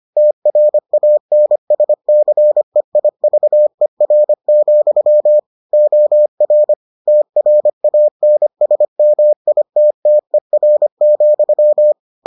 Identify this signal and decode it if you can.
Morse code, heard as a keyed tone, about 25 wpm, 600 Hz. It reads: TRANSCEIVER, OR TRANSMITTER,